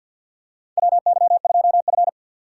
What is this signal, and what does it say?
Morse code, heard as a keyed tone, about 53 wpm, 700 Hz. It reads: WY2F
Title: USA Call Signs - 4 Characters - 6x Character Spacing - 50wpm